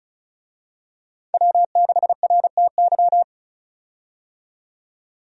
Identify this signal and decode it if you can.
Morse code, heard as a keyed tone, about 34 wpm, 700 Hz. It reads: W6RTY
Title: USA Call Signs - 5 Characters - 35wpm